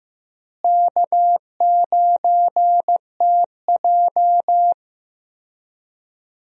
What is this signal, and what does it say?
Morse code, heard as a keyed tone, about 15 wpm, 700 Hz. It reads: K9TJ
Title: USA Call Signs - 4 Characters - Repeated 2x - 15wpm